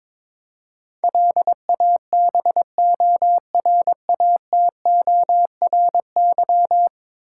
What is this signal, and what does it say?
Morse code, heard as a keyed tone, about 21 wpm, 700 Hz. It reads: LABORATORY
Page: morse-code-ninja-practice-22wpm